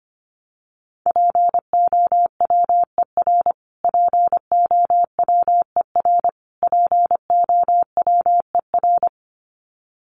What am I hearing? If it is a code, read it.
Morse code, heard as a keyed tone, about 25 wpm, 700 Hz. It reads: POWER POWER POWER
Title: Top 500 Words Repeated 25wpm